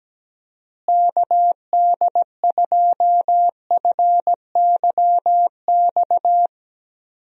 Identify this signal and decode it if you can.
Morse code, heard as a keyed tone, about 17 wpm, 700 Hz. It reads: KD2FYX
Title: USA Call Signs - 6 Characters - Repeated 2x - 17wpm